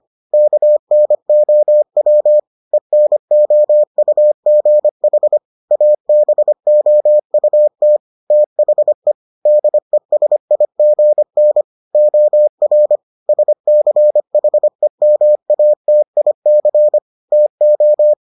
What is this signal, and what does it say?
Morse code, heard as a keyed tone, about 25 wpm, 600 Hz. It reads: KNOW ENOUGH ABOUT THE DESIGN OR SCHEMATIC TO